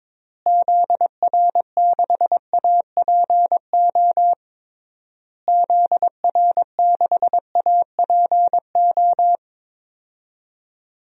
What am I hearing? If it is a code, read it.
Morse code, heard as a keyed tone, about 22 wpm, 700 Hz. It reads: ZR6APO ZR6APO
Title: DXCC-100 Call Signs - 6 Characters - Repeated 3x - 22wpm